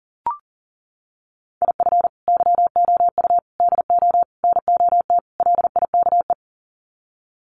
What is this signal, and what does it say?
Morse code, heard as a keyed tone, about 40 wpm, 700 Hz. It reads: IF YOU DO NOT LIKE
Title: Sets of 5 Words 40wpm